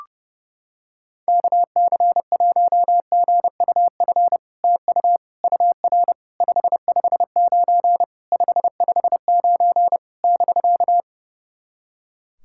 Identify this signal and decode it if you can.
Morse code, heard as a keyed tone, about 30 wpm, 700 Hz. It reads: KC1GUF TU UR 559 559 <BK>
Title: SOTA QSO Practice 30wpm